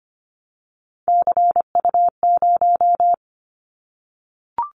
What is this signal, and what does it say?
Morse code, heard as a keyed tone, about 25 wpm, 700 Hz. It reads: CU0
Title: Callsign Prefixes for Top 100 DXCC - 25wpm